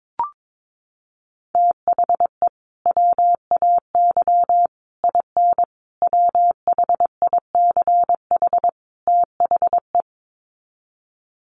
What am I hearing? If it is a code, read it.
Morse code, heard as a keyed tone, about 22 wpm, 700 Hz. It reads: THE WAY IN WHICH THE
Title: Sets of 5 Words Encore2 22wpm